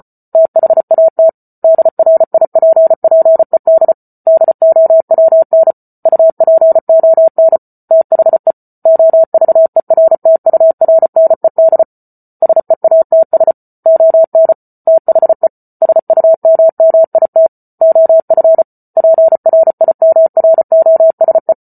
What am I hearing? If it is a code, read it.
Morse code, heard as a keyed tone, about 34 wpm, 650 Hz. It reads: THAT DRIPPED DOWN UPON THE OVERTURNED SEATS ON THE SUMMIT OF PRIMROSE
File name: Book Two / 08-Chapter 8 - 34 WPM